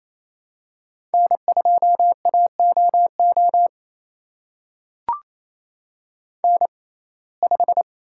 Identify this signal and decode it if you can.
Morse code, heard as a keyed tone, about 28 wpm, spaced as for 20 wpm, 700 Hz. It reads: N2AOO N5
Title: USA Call Signs - 6x Character Spacing - 28wpm